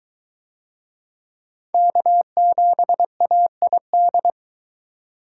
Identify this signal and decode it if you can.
Morse code, heard as a keyed tone, about 21 wpm, 700 Hz. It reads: K7AID